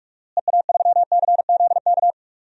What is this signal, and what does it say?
Morse code, heard as a keyed tone, about 46 wpm, 700 Hz. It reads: EA3CZK
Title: DXCC-100 Call Signs - 6 Characters - Repeated 2x - 45wpm